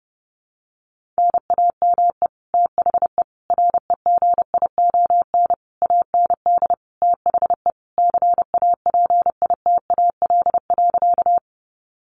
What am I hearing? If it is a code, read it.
Morse code, heard as a keyed tone, about 28 wpm, 700 Hz. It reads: NAME THE REGION AND THE CAPITAL.
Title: Sentences from Top 1500 Words 30wpm